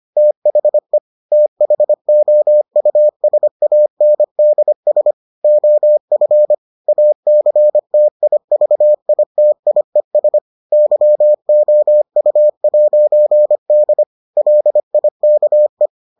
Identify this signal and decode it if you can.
Morse code, heard as a keyed tone, about 25 wpm, 600 Hz. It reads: THE THOUSANDS OF ACTIVITIES YOU'D LIKE